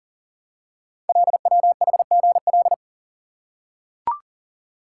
Morse code, heard as a keyed tone, about 39 wpm, 700 Hz. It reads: LWHGL
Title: Code Groups - 5 Letter:Numbers - 40wpm